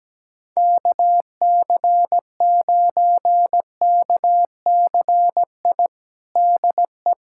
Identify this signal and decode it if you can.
Morse code, heard as a keyed tone, about 17 wpm, 700 Hz. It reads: KC9KCI DE
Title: USA Callsign DE Callsign 17wpm